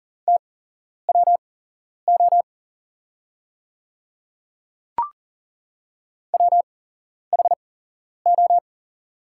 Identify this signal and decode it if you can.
Morse code, heard as a keyed tone, about 41 wpm, spaced as for 11 wpm, 700 Hz. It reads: TWO WHO
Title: 8x Character Spacing - Top 200 Words - 40wpm